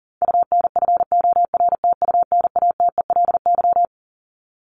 Morse code, heard as a keyed tone, about 40 wpm, 700 Hz. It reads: UNFORTUNATELY
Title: LY Suffix - Top 100 Words with Base Words 40wpm